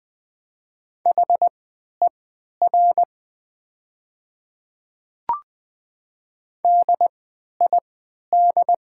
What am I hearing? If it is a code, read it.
Morse code, heard as a keyed tone, about 21 wpm, spaced as for 12 wpm, 700 Hz. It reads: HER DID
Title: 3x Character Spacing - Top 100 Words - 20wpm